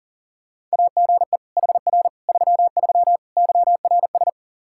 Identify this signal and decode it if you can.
Morse code, heard as a keyed tone, about 40 wpm, 700 Hz. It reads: AGE HR 33 YRS
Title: K1USN SST QSO Practice 40wpm